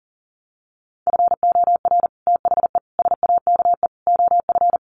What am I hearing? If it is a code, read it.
Morse code, heard as a keyed tone, about 40 wpm, 700 Hz. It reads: FOR THE SAKE OF
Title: Sets of 4 Words 40wpm